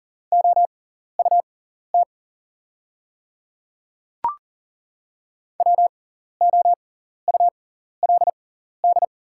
Morse code, heard as a keyed tone, about 39 wpm, spaced as for 14 wpm, 690 Hz. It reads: OUT WOULD
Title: 6x Character Spacing - Top 100 Words - 40wpm